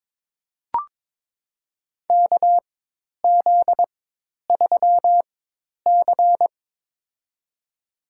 Morse code, heard as a keyed tone, about 23 wpm, spaced as for 10 wpm, 700 Hz. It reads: KZ3C
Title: USA Call Signs - 4x Character Spacing - 22wpm